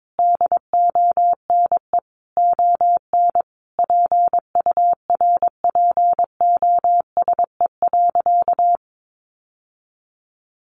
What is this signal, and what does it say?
Morse code, heard as a keyed tone, about 22 wpm, 700 Hz. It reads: DONE ON PURPOSE.
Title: Sentences from Top 800 Words 22wpm